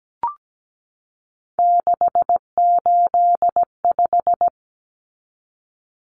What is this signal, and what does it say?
Morse code, heard as a keyed tone, about 17 wpm, 700 Hz. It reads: 685
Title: Common Cut Numbers - 3-Digit 17wpm